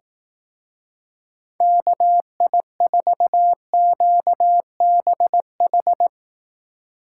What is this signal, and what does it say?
Morse code, heard as a keyed tone, about 18 wpm, 700 Hz. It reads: KI4QBH
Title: Speed Racing - USA 2x3 Call Signs 15wpm